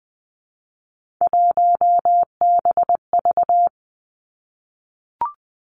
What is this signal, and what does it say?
Morse code, heard as a keyed tone, about 20 wpm, 700 Hz. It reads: 1BV
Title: Three Challenging Letter-Number Combinations 20wpm